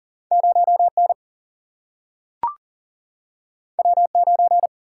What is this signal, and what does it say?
Morse code, heard as a keyed tone, about 40 wpm, 700 Hz. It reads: 0 N W 9
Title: Single-Letter Single-Number 40wpm